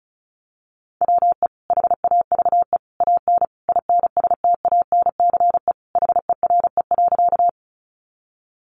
Morse code, heard as a keyed tone, about 34 wpm, 700 Hz. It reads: WE HAVE AN INSTANCE HERE.
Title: Sentences from Top 800 Words 35wpm